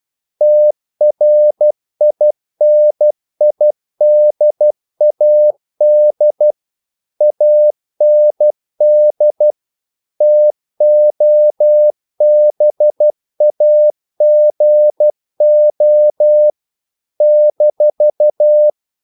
Morse code, heard as a keyed tone, about 12 wpm, 600 Hz. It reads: TRINIDAD AND TOBAGO -
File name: CW / Practice / NationalCapitals12